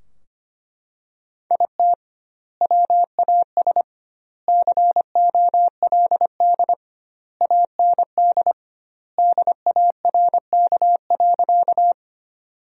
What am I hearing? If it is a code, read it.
Morse code, heard as a keyed tone, about 25 wpm, 700 Hz. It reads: IT WAS COLD AND DARK.
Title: 2x Word-Spacing - Sentences from Top 500 Words - 25wpm